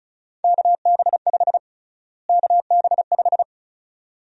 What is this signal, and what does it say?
Morse code, heard as a keyed tone, about 36 wpm, 700 Hz. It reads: KB5 KB5
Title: USA Call Sign Prefixes - Repeated 2x - 35wpm